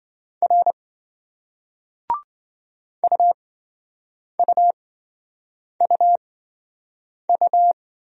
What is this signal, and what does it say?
Morse code, heard as a keyed tone, about 27 wpm, spaced as for 7 wpm, 700 Hz. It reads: R UUUU